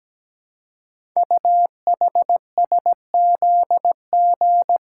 Morse code, heard as a keyed tone, about 17 wpm, 700 Hz. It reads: UHSZG
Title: Code Groups - 5 Letters - 17wpm